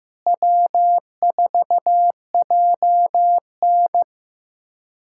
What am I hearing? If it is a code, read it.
Morse code, heard as a keyed tone, about 15 wpm, 700 Hz. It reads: W4JN
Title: USA Callsign DE Callsign - 4 Characters - 15wpm